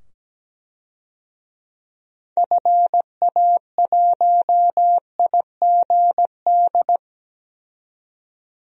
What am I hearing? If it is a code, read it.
Morse code, heard as a keyed tone, about 17 wpm, 700 Hz. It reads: FA1IGD
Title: DXCC-100 Call Signs - 6 Characters - 17wpm